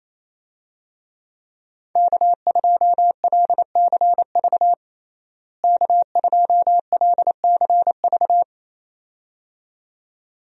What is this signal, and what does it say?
Morse code, heard as a keyed tone, about 28 wpm, 700 Hz. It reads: K2LCV K2LCV
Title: USA Call Signs - 5 Characters - Repeated 2x - 28wpm